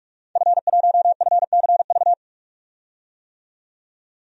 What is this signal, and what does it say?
Morse code, heard as a keyed tone, about 42 wpm, 700 Hz. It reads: F1FCV
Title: DXCC-100 Call Signs - 5 Characters - 45wpm